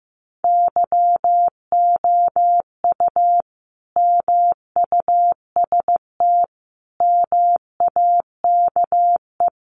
Morse code, heard as a keyed tone, about 15 wpm, 700 Hz. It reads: YOU MUST MAKE
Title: Sentences from Top 1100 Words 15wpm